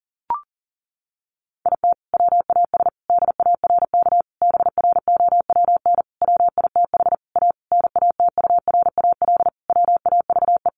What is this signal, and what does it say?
Morse code, heard as a keyed tone, about 40 wpm, 700 Hz. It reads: IT WAS DARK BROWN WITH A NATURAL WAVE
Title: Sentences from Top 1400 Words 40wpm